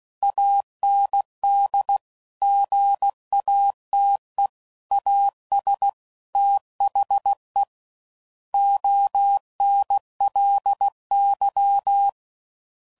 Morse code, heard as a keyed tone, about 16 wpm, 800 Hz. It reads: AND GATE AS THE ONLY
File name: CW Code Groups @15WPM / Technical Subjects / AND Gate